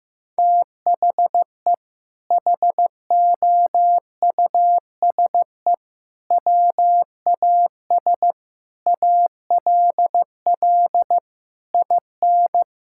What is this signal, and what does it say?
Morse code, heard as a keyed tone, about 15 wpm, 700 Hz. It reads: THE HOUSE WAS ALL IN
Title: Sentences from Top 1600 Words 15wpm